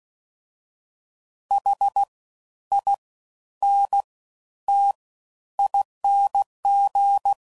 Morse code, heard as a keyed tone, about 16 wpm, spaced as for 11 wpm, 800 Hz. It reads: HINTING
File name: Suffixes / Special INGs / Fast Ings